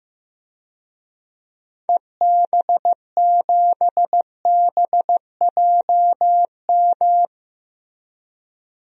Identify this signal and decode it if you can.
Morse code, heard as a keyed tone, about 15 wpm, 700 Hz. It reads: EB7BJM
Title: DXCC-100 Call Signs - 6 Characters - Repeated 3x - 15wpm